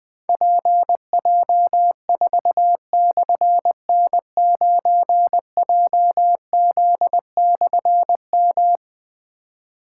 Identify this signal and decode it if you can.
Morse code, heard as a keyed tone, about 21 wpm, 700 Hz. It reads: PJ4/N9JZ/M
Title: Difficult Call Signs 20wpm